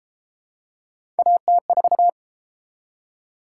Morse code, heard as a keyed tone, about 33 wpm, 700 Hz. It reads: AT4